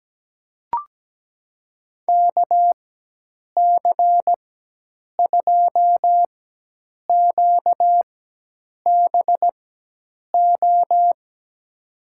Morse code, heard as a keyed tone, about 17 wpm, spaced as for 8 wpm, 700 Hz. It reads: KC2QBO
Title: USA Call Signs - 6 Characters - 4x Character Spacing - 17wpm